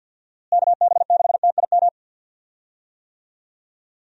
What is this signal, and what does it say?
Morse code, heard as a keyed tone, about 52 wpm, 700 Hz. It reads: KB6TIM
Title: USA Call Signs - 6 Characters - Repeated 3x - 50wpm